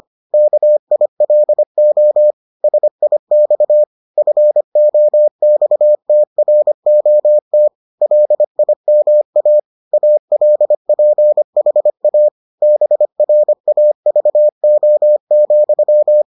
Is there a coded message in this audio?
Yes: Morse code from 0.3 to 16.2 s, heard as a keyed tone, about 25 wpm, 600 Hz. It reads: KILO SIX FOXTROT LIMA ALPHA BRAVO,